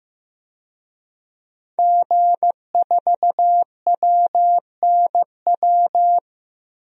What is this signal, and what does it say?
Morse code, heard as a keyed tone, about 15 wpm, 700 Hz. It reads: G4WNW
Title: DXCC-100 Call Signs - 5 Characters - Repeated 2x - 15wpm